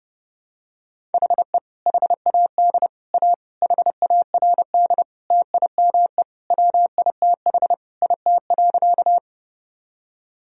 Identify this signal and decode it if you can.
Morse code, heard as a keyed tone, about 30 wpm, 700 Hz. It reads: HE HAD A HARD TIME WITH IT.
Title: Sentences from Top 1000 Words - Review Part 1 - 30wpm